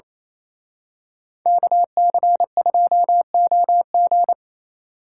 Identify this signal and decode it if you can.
Morse code, heard as a keyed tone, about 28 wpm, 700 Hz. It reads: KC2OG
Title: Speed Racing - USA 2x2 Call Signs 28wpm